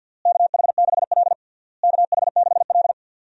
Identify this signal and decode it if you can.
Morse code, heard as a keyed tone, about 46 wpm, 700 Hz. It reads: KH6L KH6L
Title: DXCC-100 Call Signs - 4 Characters - Repeated 3x - 50wpm